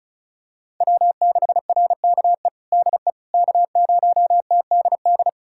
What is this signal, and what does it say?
Morse code, heard as a keyed tone, about 34 wpm, 700 Hz. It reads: W6RKE DE K0TDD
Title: USA Callsign DE Callsign - 5 Characters - 35wpm